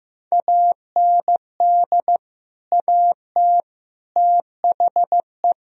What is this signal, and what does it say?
Morse code, heard as a keyed tone, about 15 wpm, 700 Hz. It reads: AND AT THE
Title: Speed-Racing - Sets of 5 Words 15wpm